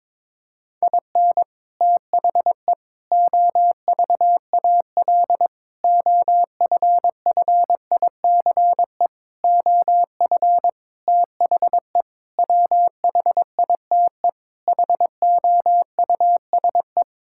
Morse code, heard as a keyed tone, about 22 wpm, 700 Hz. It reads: IN THE OVAL OFFICE OF THE WHITE HOUSE
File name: practice / 22wpm / Sets of 8 Words - 22wpm